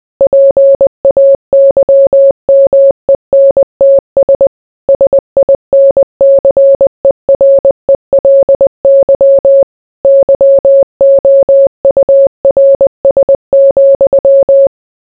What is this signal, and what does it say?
Morse code, heard as a keyed tone, about 20 wpm, 550 Hz. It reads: PAYMENTS SINCERELY YOURS,